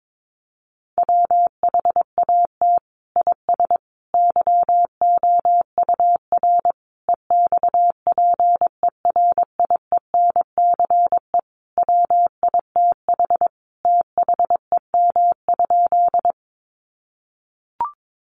Morse code, heard as a keyed tone, about 22 wpm, 700 Hz. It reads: WHAT IS YOUR EXPERIENCE WITH THEM?
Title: Sentences from Top 600 Words 22wpm